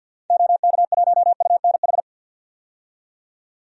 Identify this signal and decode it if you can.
Morse code, heard as a keyed tone, about 49 wpm, 700 Hz. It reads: OK1UNH
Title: DXCC-100 Call Signs - 6 Characters - 50wpm